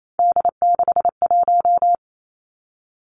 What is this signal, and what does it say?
Morse code, heard as a keyed tone, about 28 wpm, 690 Hz. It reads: D61
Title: Three Challenging Letter-Number Combinations 28wpm